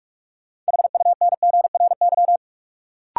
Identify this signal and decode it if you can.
Morse code, heard as a keyed tone, about 46 wpm, 700 Hz. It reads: HUNGRY
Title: Sentences from Top 1400 Words - New Word Drill Alt - 45wpm